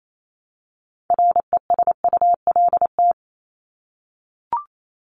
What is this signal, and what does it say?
Morse code, heard as a keyed tone, about 28 wpm, 700 Hz. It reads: RESULT
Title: ING Suffix - Top 100 Words with Base Words 28wpm